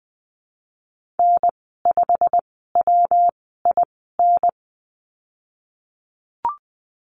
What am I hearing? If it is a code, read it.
Morse code, heard as a keyed tone, about 20 wpm, spaced as for 15 wpm, 700 Hz. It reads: N5WIN
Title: 154 Lesson 25g - Letter p - Call Signs 20wpm